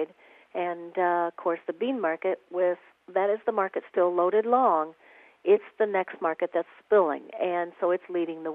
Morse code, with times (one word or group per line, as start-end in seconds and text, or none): none